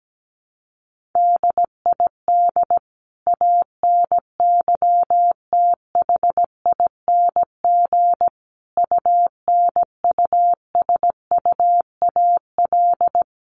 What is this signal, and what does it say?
Morse code, heard as a keyed tone, about 17 wpm, 700 Hz. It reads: DID ANYTHING UNUSUAL
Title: Sentences from Top 1400 Words 17wpm